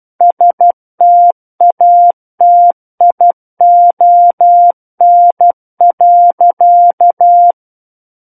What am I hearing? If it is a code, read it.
Morse code, heard as a keyed tone, about 12 wpm, 700 Hz. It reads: STATION.
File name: Book One / 03-Chapter 3 - 12 WPM